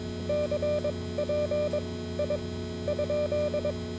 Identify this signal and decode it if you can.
Morse code, heard as a keyed tone, about 22 wpm, spaced as for 14 wpm, 590 Hz, with noise. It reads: CPI?